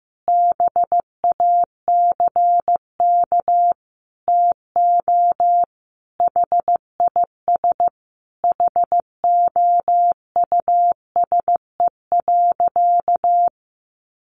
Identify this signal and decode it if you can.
Morse code, heard as a keyed tone, about 15 wpm, 700 Hz. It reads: BACK TO HIS HOUSE.
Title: Easy Phrases 15wpm